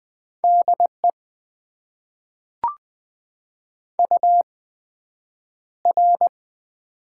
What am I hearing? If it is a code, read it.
Morse code, heard as a keyed tone, about 20 wpm, spaced as for 9 wpm, 700 Hz. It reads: DE UR
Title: Top 90 QSO Elements - 8x Character Spacing - 20wpm